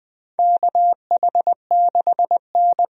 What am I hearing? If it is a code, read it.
Morse code, heard as a keyed tone, about 20 wpm, 700 Hz. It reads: KH6N
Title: DXCC-100 Call Signs - 4 Characters - 20wpm